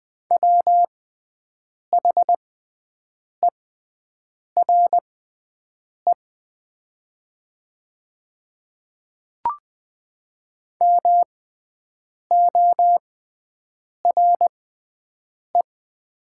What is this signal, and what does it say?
Morse code, heard as a keyed tone, about 20 wpm, spaced as for 7 wpm, 700 Hz. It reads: WHERE MORE
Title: 6x Character Spacing - Top 300 Words - 20wpm